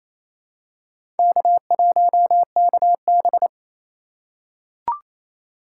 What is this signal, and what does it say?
Morse code, heard as a keyed tone, about 28 wpm, 700 Hz. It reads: K1KB
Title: USA Call Signs - 4 Characters - 3x Character Spacing - 28wpm